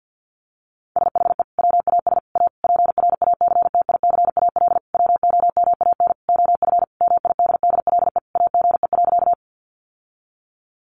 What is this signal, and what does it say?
Morse code, heard as a keyed tone, about 50 wpm, 700 Hz. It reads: SHE WAS A PRACTICAL WOMAN OF MIDDLE AGE.
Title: Sentences from Top 1400 Words 50wpm